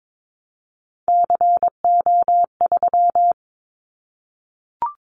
Morse code, heard as a keyed tone, about 22 wpm, 700 Hz. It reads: CO3
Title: Callsign Prefixes for Top 100 DXCC - 22wpm